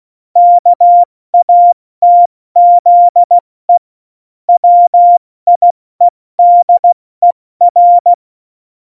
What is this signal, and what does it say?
Morse code, heard as a keyed tone, about 16 wpm, 700 Hz. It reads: KATZE WIEDER
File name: Morsekurs / Witze / Witz 16. Januar 2018, 70 F80